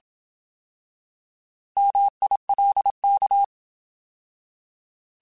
Morse code, heard as a keyed tone, about 27 wpm, 800 Hz. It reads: MILK
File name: Alphabetical / Alphabetical mp3s / Ms @25